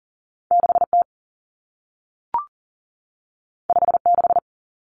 Morse code, heard as a keyed tone, about 40 wpm, 700 Hz. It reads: 6T 56
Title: Common Cut Numbers - 2-Digit 40wpm